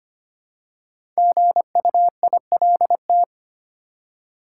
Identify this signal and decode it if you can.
Morse code, heard as a keyed tone, about 25 wpm, 700 Hz. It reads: GUILT
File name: 5 Letter Words - First 3 Letters Given - 25wpm